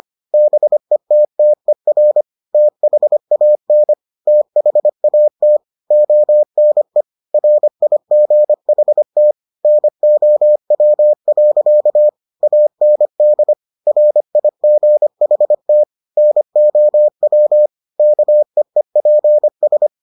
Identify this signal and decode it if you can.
Morse code, heard as a keyed tone, about 25 wpm, 600 Hz. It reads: BETTER THAN THAT ONE RIGHT NOW. AND RIGHT NOW KEEPS